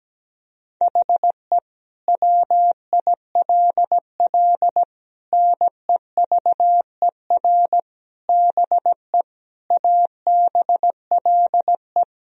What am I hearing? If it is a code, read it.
Morse code, heard as a keyed tone, about 17 wpm, 700 Hz. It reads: HE WILL NEVER BE ABLE
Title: Sentences from Top 1000 Words - Review Part 2 - 17wpm